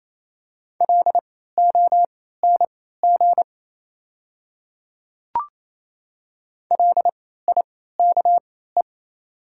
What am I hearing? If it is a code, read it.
Morse code, heard as a keyed tone, about 28 wpm, spaced as for 16 wpm, 700 Hz. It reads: LONG LIKE
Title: 3x Character Spacing - Top 100 Words - 28wpm